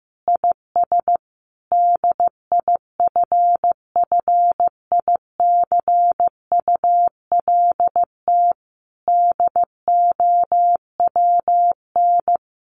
Morse code, heard as a keyed tone, about 15 wpm, 700 Hz. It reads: IS DIFFICULT DOWN
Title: Sentences from Top 1100 Words 15wpm